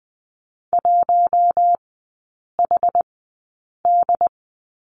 Morse code, heard as a keyed tone, about 20 wpm, spaced as for 8 wpm, 700 Hz. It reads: 1HD
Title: 205 Lesson 33e - Letter j - Three Characters 20wpm